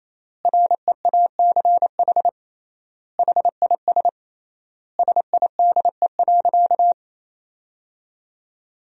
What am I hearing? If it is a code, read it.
Morse code, heard as a keyed tone, about 28 wpm, 700 Hz. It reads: REACH HIS SIDE.
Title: 3x Word-Spacing - Sentences from Top 400 Words - 28wpm